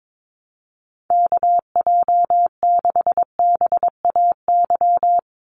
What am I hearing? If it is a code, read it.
Morse code, heard as a keyed tone, about 22 wpm, 700 Hz. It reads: KJ6BAY
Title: USA Call Signs 22wpm